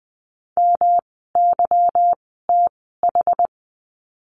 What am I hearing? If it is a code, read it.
Morse code, heard as a keyed tone, about 20 wpm, spaced as for 14 wpm, 700 Hz. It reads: MYTH
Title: 146 Lesson 24f - Letter y - Words 20wpm